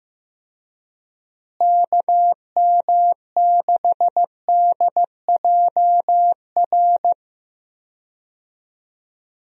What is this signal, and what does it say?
Morse code, heard as a keyed tone, about 15 wpm, 700 Hz. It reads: KM6DJR
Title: USA Call Signs - 6 Characters - Repeated 3x - 15wpm